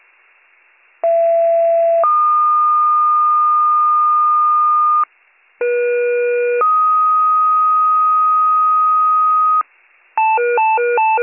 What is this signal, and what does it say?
Two-tone page, heard as a one-second tone, then a three-second tone, about 670 then 1220 Hz, about 480 then 1250 Hz. It is heard from 1.0 to 9.6 s.